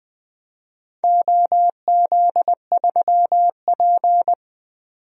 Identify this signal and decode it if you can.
Morse code, heard as a keyed tone, about 21 wpm, 700 Hz. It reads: OZ3P
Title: DXCC-100 Call Signs - 4 Characters - 20wpm